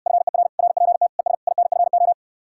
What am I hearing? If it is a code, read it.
Morse code, heard as a keyed tone, about 67 wpm, 700 Hz. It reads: FER RPT ES INFO